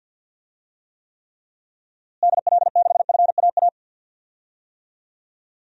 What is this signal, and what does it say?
Morse code, heard as a keyed tone, about 46 wpm, 700 Hz. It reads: DL6FAA
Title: DXCC-100 Call Signs - 6 Characters - Repeated 3x - 50wpm